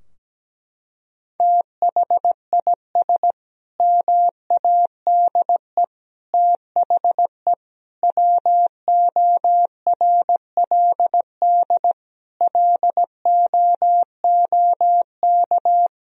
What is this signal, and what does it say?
Morse code, heard as a keyed tone, about 17 wpm, 700 Hz. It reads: THIS MADE THE WORLD LOOK
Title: Sentences from Top 1000 Words - Review Part 2 - 17wpm